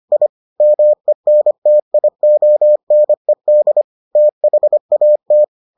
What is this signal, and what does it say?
Morse code, heard as a keyed tone, about 25 wpm, 600 Hz. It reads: I MENTIONED THAT